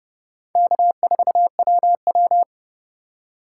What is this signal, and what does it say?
Morse code, heard as a keyed tone, about 37 wpm, 700 Hz. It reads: K4WW
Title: Speed Racing - USA 1x2 Call Signs 30wpm